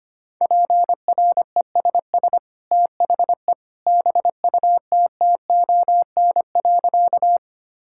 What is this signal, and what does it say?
Morse code, heard as a keyed tone, about 25 wpm, 700 Hz. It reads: PRESS THE BUTTON.
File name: Easy Sentences with Hints - 25wpm